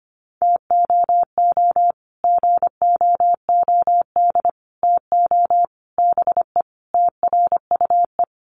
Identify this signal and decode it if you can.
Morse code, heard as a keyed tone, about 25 wpm, 700 Hz. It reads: TOO GOOD TO BE TRUE
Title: Sets of 5 Words Encore 25wpm